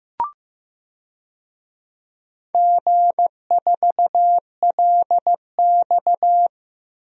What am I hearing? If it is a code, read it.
Morse code, heard as a keyed tone, about 15 wpm, 700 Hz. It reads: G4LX
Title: DXCC-100 Call Signs - 4 Characters - 15wpm